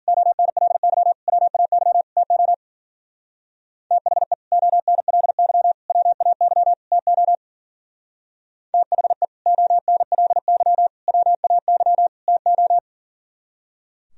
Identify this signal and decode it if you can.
Morse code, heard as a keyed tone, about 51 wpm, 700 Hz. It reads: ONLY WAY TO THE ONLY WAY TO THE ONLY WAY TO